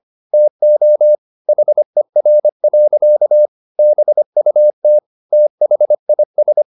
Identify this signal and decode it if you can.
Morse code, heard as a keyed tone, about 25 wpm, 600 Hz. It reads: TO HER. BUT THIS